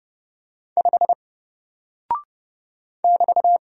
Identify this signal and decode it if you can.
Morse code, heard as a keyed tone, about 30 wpm, 700 Hz. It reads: 5 <BT>
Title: Single Character (Letter:Number:Punctuation:Prosign) Alt 30wpm